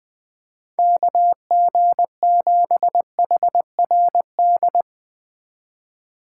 Morse code, heard as a keyed tone, about 20 wpm, 700 Hz. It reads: KG7HRD